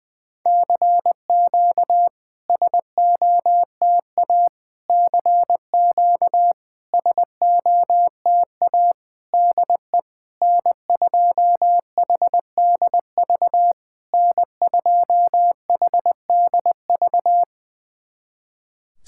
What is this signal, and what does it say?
Morse code, heard as a keyed tone, about 21 wpm, 700 Hz. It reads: CQ SOTA CQ SOTA DE N2HDV N2HDV
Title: SOTA QSO Practice 20wpm